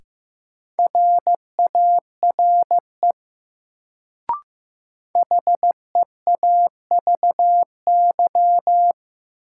Morse code, heard as a keyed tone, about 15 wpm, 700 Hz. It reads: RARE HEAVY
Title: Sentences from Top 1100 Words - New Word Drill Alt - 15wpm